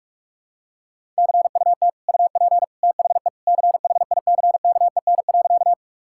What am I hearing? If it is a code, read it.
Morse code, heard as a keyed tone, about 46 wpm, 700 Hz. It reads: CUT UP THE CHICKEN.
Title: Sentences from Top 1000 Words - Review Part 5 - 45wpm